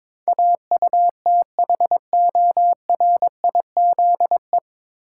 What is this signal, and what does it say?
Morse code, heard as a keyed tone, about 22 wpm, 700 Hz. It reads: AUTHORIZE
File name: 9 Letter Words - First 3 Letters Given - 22wpm